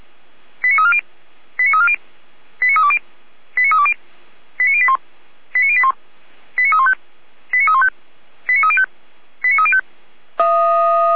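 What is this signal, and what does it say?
Five-tone selective call (ZVEI), heard as quick runs of five tones, tones of about 70 ms. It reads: 89340 89340 89320 89320 89081 89081 89315 89315 89305 89305
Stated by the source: ZVEI five-tone call 89502 received on SDR by feuerwehr-orlamuende (fire brigade Orlamünde).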